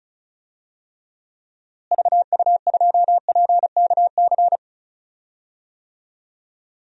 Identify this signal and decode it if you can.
Morse code, heard as a keyed tone, about 34 wpm, 700 Hz. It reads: VU2PKC
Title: DXCC-100 Call Signs - 6 Characters - Repeated 3x - 35wpm